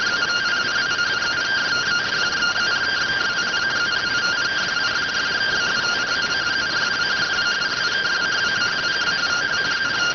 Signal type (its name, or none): two-tone teleprinter signal